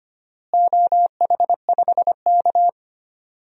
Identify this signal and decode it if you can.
Morse code, heard as a keyed tone, about 25 wpm, 700 Hz. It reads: OH5K
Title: DXCC-100 Call Signs - 4 Characters - Repeated 2x - 25wpm